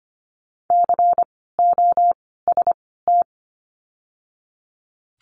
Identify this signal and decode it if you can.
Morse code, heard as a keyed tone, about 25 wpm, 700 Hz. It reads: C O S T
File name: course / 25:10wpm / 094 Lesson 16f - Letter c - Words 25wpm